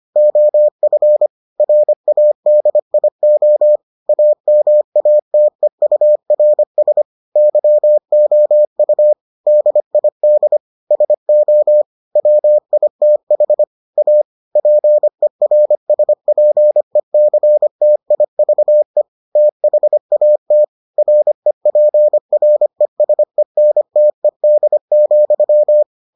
Morse code, heard as a keyed tone, about 25 wpm, 600 Hz. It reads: OF RADIO AMATEURS YOU DID SO WITH A PERSPECTIVE THAT REPRESENTED,